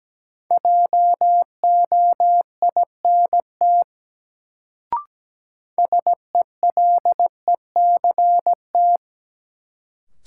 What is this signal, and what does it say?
Morse code, heard as a keyed tone, about 17 wpm, 700 Hz. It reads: JOINT SELECT
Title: Sentences from Top 1300 Words - New Word Drill - 17wpm